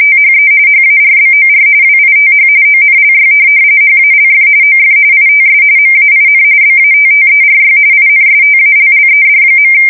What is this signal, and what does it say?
RTTY, about 45.45 baud. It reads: THE HARBOR WITH ONLY HALF OF ITS MAIN BATTERY INSTALLED.
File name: fsk170 / USS-Massachusetts